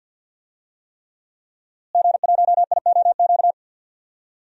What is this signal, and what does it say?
Morse code, heard as a keyed tone, about 48 wpm, 700 Hz. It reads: G1IOQ
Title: DXCC-100 Call Signs - 5 Characters - Repeated 3x - 50wpm